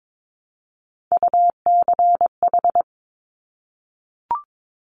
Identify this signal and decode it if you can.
Morse code, heard as a keyed tone, about 22 wpm, 700 Hz. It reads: UCH
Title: Top Three Letter Combinations in Words 22wpm